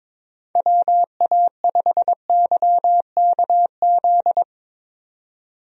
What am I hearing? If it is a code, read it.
Morse code, heard as a keyed tone, about 22 wpm, 700 Hz. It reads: WA5YKZ
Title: USA Call Signs - 6 Characters - 22wpm